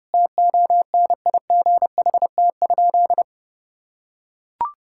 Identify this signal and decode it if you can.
Morse code, heard as a keyed tone, about 30 wpm, 700 Hz. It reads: TONIGHT?
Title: Sentences from Top 1000 Words - Review Part 5 - 30wpm